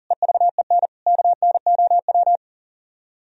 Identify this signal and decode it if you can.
Morse code, heard as a keyed tone, about 46 wpm, 700 Hz. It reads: EVEN KNOW
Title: Speed-Racing - Sets of 5 Words 40wpm